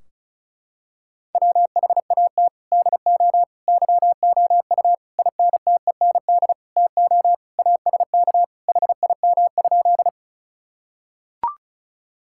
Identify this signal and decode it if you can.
Morse code, heard as a keyed tone, about 34 wpm, 700 Hz. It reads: WHAT DO YOU INTEND TO ASK HIM?
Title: Sentences from Top 1600 Words 35wpm